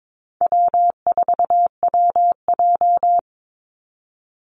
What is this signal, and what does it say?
Morse code, heard as a keyed tone, about 22 wpm, 700 Hz. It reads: W4WJ
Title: Easy Call Signs - 22wpm